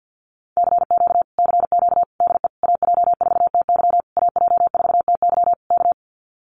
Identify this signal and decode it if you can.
Morse code, heard as a keyed tone, about 48 wpm, 700 Hz. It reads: CQ CQ DE AJ4TY AJ4TY K
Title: Common QSO Phrases Encore 50wpm